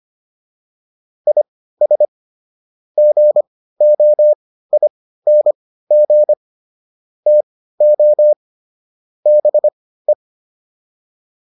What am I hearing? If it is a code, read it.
Morse code, heard as a keyed tone, about 25 wpm, spaced as for 15 wpm, 600 Hz. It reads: IS GOING TO BE
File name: WD305 15